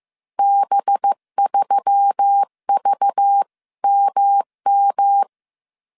Morse code, heard as a keyed tone, about 15 wpm, 800 Hz. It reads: B3VMM
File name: CW / DET / 1 6 1500C S B 14wpm 2958